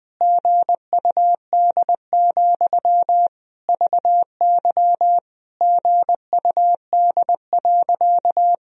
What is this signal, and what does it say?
Morse code, heard as a keyed tone, about 20 wpm, 700 Hz. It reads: GUD, VY GUD.
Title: Sentences from Top 200 Words and Abbreviations 20wpm